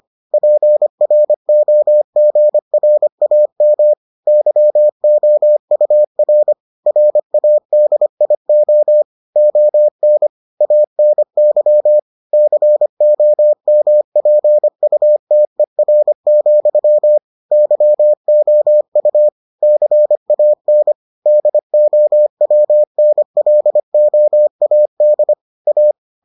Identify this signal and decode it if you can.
Morse code, heard as a keyed tone, about 25 wpm, 600 Hz. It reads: PROGRAM YOUR RADIO ON ANY COMPUTER, YOU CAN DOWNLOAD A